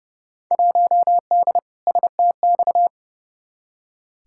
Morse code, heard as a keyed tone, about 32 wpm, 700 Hz. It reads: 1D STX
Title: Field Day Exchanges - 30wpm